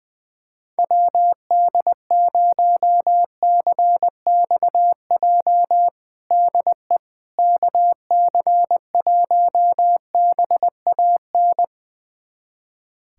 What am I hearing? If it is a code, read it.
Morse code, heard as a keyed tone, about 20 wpm, 700 Hz. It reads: WD0CXJ DE KC1BAN
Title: USA Callsign DE Callsign - 6 Characters - 20wpm